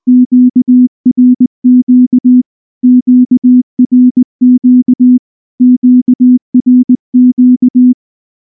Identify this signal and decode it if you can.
Morse code, heard as a keyed tone, about 19 wpm, 250 Hz. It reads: QRQ QRQ QRQ
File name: morse / morse-training-20WPM-5